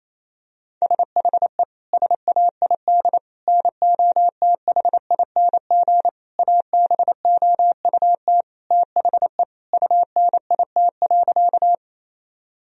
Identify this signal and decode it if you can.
Morse code, heard as a keyed tone, about 28 wpm, 700 Hz. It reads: SHE SAID NOTHING ABOUT THE UNIT.
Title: Sentences from Top 1000 Words - Review Part 2 - 28wpm